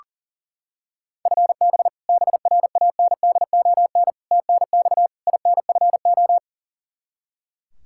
Morse code, heard as a keyed tone, about 40 wpm, 700 Hz. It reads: FB BRANDON TNX INFO
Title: Practice QSO - Exchange 3 - More Info - 40wpm